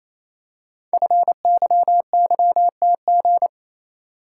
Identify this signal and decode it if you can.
Morse code, heard as a keyed tone, about 28 wpm, 700 Hz. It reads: FYYTG
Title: Code Groups - 5 Letters - 28wpm